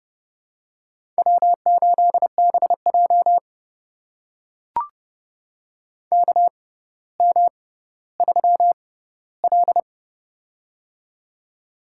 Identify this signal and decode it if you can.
Morse code, heard as a keyed tone, about 30 wpm, spaced as for 15 wpm, 700 Hz. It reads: W8BJ KM3L
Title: USA Call Signs - 4 Characters - 6x Character Spacing - 30wpm